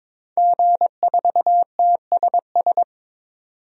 Morse code, heard as a keyed tone, about 22 wpm, 700 Hz. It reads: G4TSS
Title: DXCC-100 Call Signs - 5 Characters - 22wpm